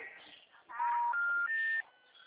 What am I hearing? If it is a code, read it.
Special information tone.